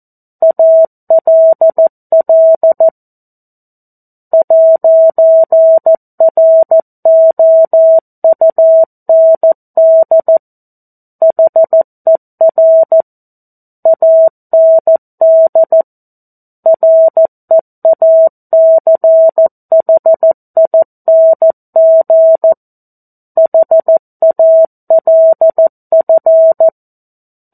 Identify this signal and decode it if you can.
Morse code, heard as a keyed tone, about 14 wpm, 650 Hz. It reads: ALL 'ROUND HER AND REACHING HALF
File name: 02-Chapter-02 14 WPM